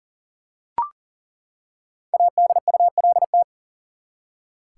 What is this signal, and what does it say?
Morse code, heard as a keyed tone, about 40 wpm, 700 Hz. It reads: ADULT
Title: Five Letter Words - 40wpm